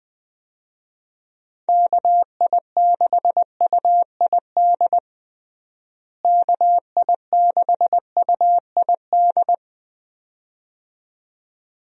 Morse code, heard as a keyed tone, about 20 wpm, 700 Hz. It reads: KI6UID KI6UID
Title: USA Call Signs - 6 Characters - Repeated 2x - 20wpm